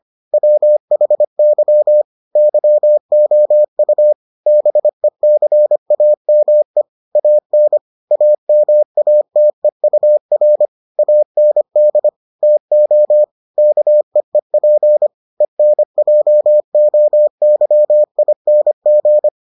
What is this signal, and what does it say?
Morse code, heard as a keyed tone, about 25 wpm, 600 Hz. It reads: WHY YOU BECAME AN AMATEUR AND TO KEEP ENJOYING